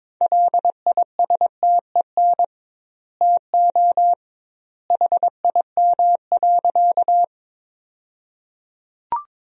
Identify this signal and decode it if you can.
Morse code, heard as a keyed tone, about 22 wpm, 700 Hz. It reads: LISTEN TO HIM.
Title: 2x Word-Spacing - Sentences from Top 400 Words - 22wpm